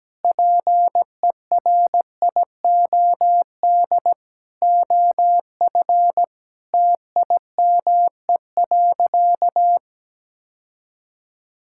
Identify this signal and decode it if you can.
Morse code, heard as a keyed tone, about 17 wpm, 700 Hz. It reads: PERIOD OF TIME.
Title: Sentences from Top 1000 Words - Review Part 5 - 17wpm